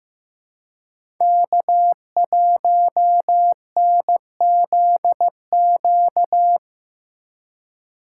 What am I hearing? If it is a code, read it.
Morse code, heard as a keyed tone, about 15 wpm, 700 Hz. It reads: K1NZQ
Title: USA Call Signs - 5 Characters - 3x Character Spacing - 15wpm